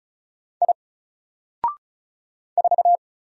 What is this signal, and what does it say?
Morse code, heard as a keyed tone, about 36 wpm, 700 Hz. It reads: I 4
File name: Single-Letter-Number-35wpm